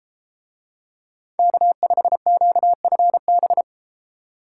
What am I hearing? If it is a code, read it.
Morse code, heard as a keyed tone, about 32 wpm, 700 Hz. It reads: K5QFB